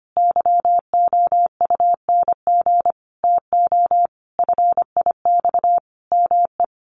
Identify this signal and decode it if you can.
Morse code, heard as a keyed tone, about 25 wpm, 700 Hz. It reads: YOUNG TO FIX ME
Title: Sentences from Top 1400 Words 25wpm